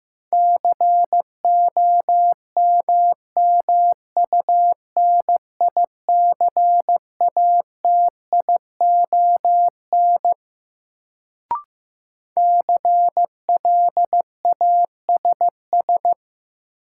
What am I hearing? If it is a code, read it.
Morse code, heard as a keyed tone, about 15 wpm, 700 Hz. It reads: COMMUNICATION CLASS
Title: Sentences from Top 1400 Words - New Word Drill Alt - 15wpm